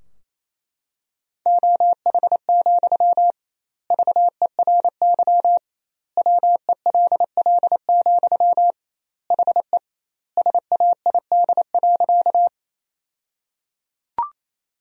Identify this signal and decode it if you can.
Morse code, heard as a keyed tone, about 28 wpm, 700 Hz. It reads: OH, VERY WELL, HE SAID.
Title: 2x Word-Spacing - Sentences from Top 500 Words - 28wpm